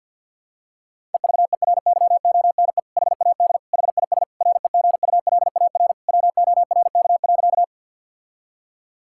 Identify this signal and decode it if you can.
Morse code, heard as a keyed tone, about 52 wpm, 700 Hz. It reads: EVERYONE HAD HIS REGULAR WORK.
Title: Sentences from Top 1000 Words - Review Part 6 - 50wpm